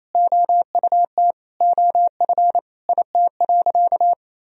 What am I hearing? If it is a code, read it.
Morse code, heard as a keyed tone, about 28 wpm, 700 Hz. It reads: OUT OF IT.
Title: Sentences from Top 1000 Words - Review Part 1 - 28wpm